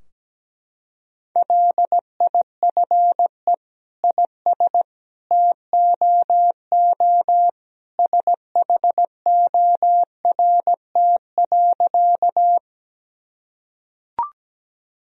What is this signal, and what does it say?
Morse code, heard as a keyed tone, about 17 wpm, 700 Hz. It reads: LIFE IS TOO SHORT.
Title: Sentences from Top 1000 Words - Review Part 2 - 17wpm